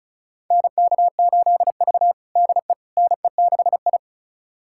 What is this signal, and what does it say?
Morse code, heard as a keyed tone, about 36 wpm, 700 Hz. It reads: NK8V DE NE6I
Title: USA Callsign DE Callsign - 4 Characters - 35wpm